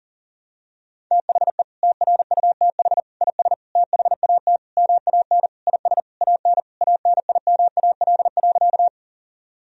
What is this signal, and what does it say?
Morse code, heard as a keyed tone, about 40 wpm, 700 Hz. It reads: THE TRUTH IS THAT MAN IS AN ANIMAL.
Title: Sentences from Top 1000 Words - Review Part 1 - 40wpm